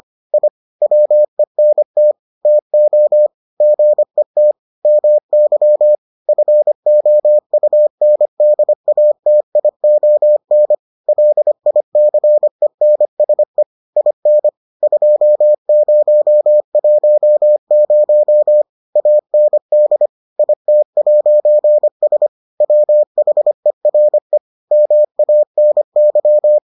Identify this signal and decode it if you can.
Morse code, heard as a keyed tone, about 25 wpm, 600 Hz. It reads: I WENT TO GET MY FOUNDATION LICENSE IN 2010 AND IT'S WHERE MANY